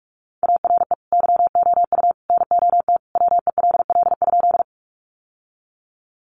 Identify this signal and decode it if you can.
Morse code, heard as a keyed tone, about 44 wpm, 700 Hz. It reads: ARE YOU NOT WELL?
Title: Sentences from Top 600 Words 45wpm